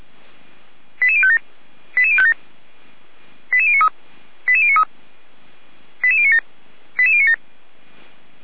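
Five-tone selective call (ZVEI), heard as quick runs of five tones, tones of about 70 ms. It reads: 88057 88057 88093 88093 88097 88097
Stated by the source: ZVEI five-tone call received on an SDR, from the audio archive of Feuerwehr Orlamünde.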